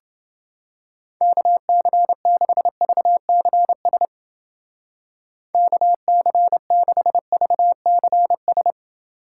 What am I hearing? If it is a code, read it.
Morse code, heard as a keyed tone, about 29 wpm, 700 Hz. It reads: KC6VCS KC6VCS